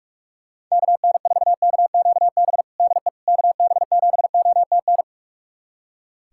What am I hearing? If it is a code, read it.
Morse code, heard as a keyed tone, about 41 wpm, 690 Hz. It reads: KN4KQB DE KB7OTN
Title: USA Callsign DE Callsign 45wpm